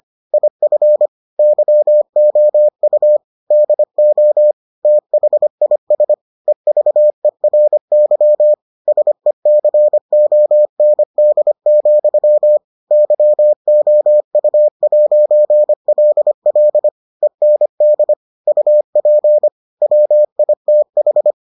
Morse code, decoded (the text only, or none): IF YOU DO THIS EVERY SECOND, YOU'LL END UP WITH